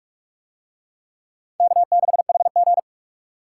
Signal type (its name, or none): Morse code